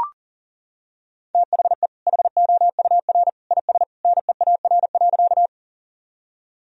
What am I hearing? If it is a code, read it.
Morse code, heard as a keyed tone, about 40 wpm, 700 Hz. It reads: THE HOUR IS NEAR.
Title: Sentences from Top 1000 Words - Review Part 1 - 40wpm